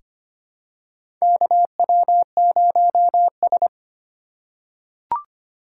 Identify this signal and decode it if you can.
Morse code, heard as a keyed tone, about 25 wpm, 700 Hz. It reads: KW0S
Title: USA Call Signs - 4x Character Spacing - 25wpm